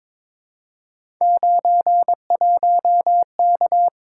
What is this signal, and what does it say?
Morse code, heard as a keyed tone, about 22 wpm, 700 Hz. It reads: 91K
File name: practice / 22wpm / Three Letter-Number Combinations - 22wpm